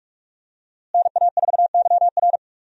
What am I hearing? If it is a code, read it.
Morse code, heard as a keyed tone, about 46 wpm, 700 Hz. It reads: NA4YR
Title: USA Call Signs - 5 Characters - Repeated 3x - 45wpm